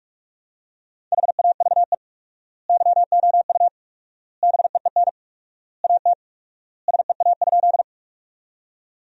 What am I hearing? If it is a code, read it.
Morse code, heard as a keyed tone, about 42 wpm, 700 Hz. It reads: HAVE YOU BEEN AT SEA?
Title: 4x Word-Spacing - Sentences from Top 300 Words - 45wpm